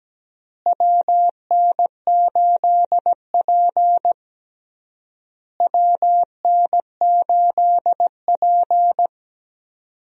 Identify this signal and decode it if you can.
Morse code, heard as a keyed tone, about 17 wpm, 700 Hz. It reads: WN8P WN8P
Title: USA Call Signs - 4 Characters - Repeated 3x - 17wpm